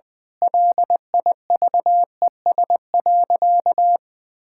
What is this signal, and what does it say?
Morse code, heard as a keyed tone, about 21 wpm, 700 Hz. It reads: LIVES.
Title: Difficult Sentences with Hints - 20wpm